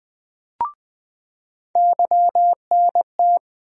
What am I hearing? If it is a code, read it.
Morse code, heard as a keyed tone, about 21 wpm, 700 Hz. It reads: YNT
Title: USA Call Sign Postfixes - 3 Letters 20wpm